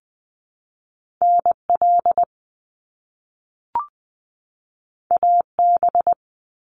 Morse code, heard as a keyed tone, about 20 wpm, 700 Hz. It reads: NL AB
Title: Canadian Provinces and Territories 20wpm